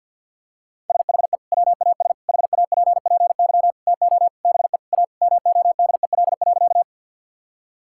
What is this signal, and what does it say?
Morse code, heard as a keyed tone, about 50 wpm, 700 Hz. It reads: SHE WAS HAPPY TO BE A MODEL.
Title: Sentences from Top 1000 Words - Review Part 5 - 50wpm